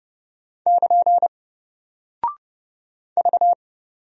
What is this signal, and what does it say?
Morse code, heard as a keyed tone, about 30 wpm, 700 Hz. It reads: <KN> V
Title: Single Character (Letter:Number:Punctuation:Prosign) Alt 30wpm